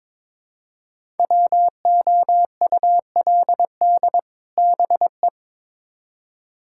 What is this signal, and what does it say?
Morse code, heard as a keyed tone, about 22 wpm, 700 Hz. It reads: WOULD BE
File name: Build A Phrase - 2-Word Phrases - 22wpm